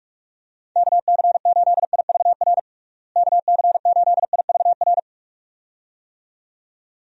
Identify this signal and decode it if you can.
Morse code, heard as a keyed tone, about 41 wpm, 700 Hz. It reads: KC8IVR KC8IVR
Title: USA Call Signs - 6 Characters - Repeated 2x - 45wpm